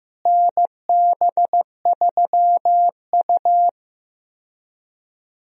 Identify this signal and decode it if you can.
Morse code, heard as a keyed tone, about 15 wpm, 700 Hz. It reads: NB3U
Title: USA Call Signs - 4 Characters - Repeated 2x - 15wpm